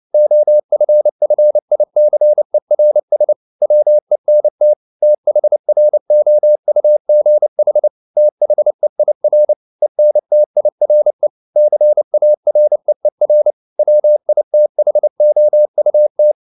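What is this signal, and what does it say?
Morse code, heard as a keyed tone, about 29 wpm, 600 Hz. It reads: OFFICERS WENT THROUGH THEIR ENTIRE CAREER WITHOUT
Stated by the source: Morse code at 29 wpm, sending Ken Follett's novel Never in English, from CW med Gnister.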